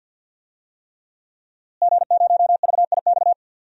Morse code, heard as a keyed tone, about 52 wpm, 700 Hz. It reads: G0VIX
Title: DXCC-100 Call Signs - 5 Characters - Repeated 3x - 50wpm